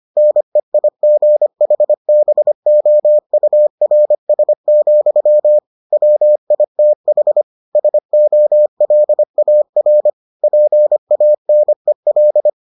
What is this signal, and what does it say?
Morse code, heard as a keyed tone, about 25 wpm, 600 Hz. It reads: NEIGHBOURS, WITH SOLAR PANEL